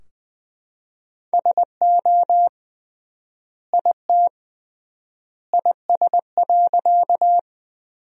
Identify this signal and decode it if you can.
Morse code, heard as a keyed tone, about 21 wpm, 700 Hz. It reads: SO IT IS.
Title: 3x Word-Spacing - Sentences from Top 100 Words - 20wpm